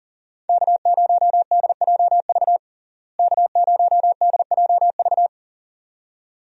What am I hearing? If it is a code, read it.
Morse code, heard as a keyed tone, about 40 wpm, 700 Hz. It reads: K0DJV K0DJV
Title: USA Call Signs - 5 Characters - Repeated 2x - 40wpm